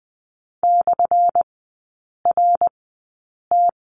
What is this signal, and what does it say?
Morse code, heard as a keyed tone, about 20 wpm, spaced as for 8 wpm, 700 Hz. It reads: /RT
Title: 218 Lesson 35e - Number 0 - Three Characters 20wpm